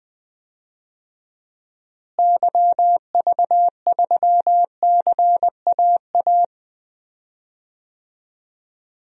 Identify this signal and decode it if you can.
Morse code, heard as a keyed tone, about 20 wpm, 700 Hz. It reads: YV3CAA
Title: DXCC-100 Call Signs - 6 Characters - Repeated 2x - 20wpm